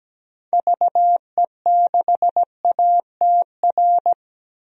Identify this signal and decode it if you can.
Morse code, heard as a keyed tone, about 17 wpm, 700 Hz. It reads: VE6ATR
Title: DXCC-100 Call Signs - 6 Characters - 17wpm